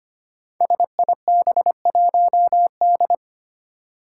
Morse code, heard as a keyed tone, about 25 wpm, 700 Hz. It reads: SIB1D
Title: Five Challenging Letter-Number Combinations 25wpm